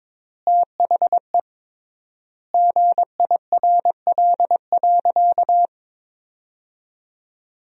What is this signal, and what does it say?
Morse code, heard as a keyed tone, about 22 wpm, 700 Hz. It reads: THE GIRL.
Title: 3x Word-Spacing - Sentences from Top 400 Words - 22wpm